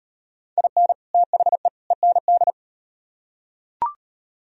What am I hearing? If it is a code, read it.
Morse code, heard as a keyed tone, about 38 wpm, 700 Hz. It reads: IN THE END